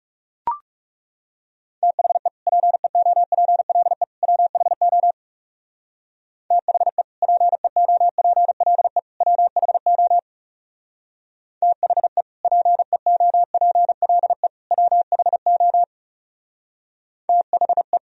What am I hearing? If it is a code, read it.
Morse code, heard as a keyed tone, about 39 wpm, 700 Hz. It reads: THE PEOPLE WHO THE PEOPLE WHO THE PEOPLE WHO THE